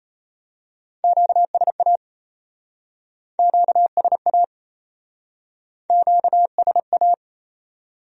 Morse code, heard as a keyed tone, about 34 wpm, 700 Hz. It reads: QSA QSA QSA
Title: Speed-Racing Q-Codes - 28wpm